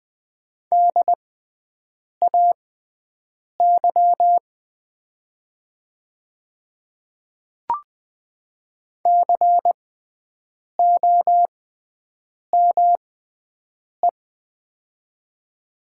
Morse code, heard as a keyed tone, about 20 wpm, spaced as for 7 wpm, 700 Hz. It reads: DAY COME
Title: 6x Character Spacing - Top 100 Words - 20wpm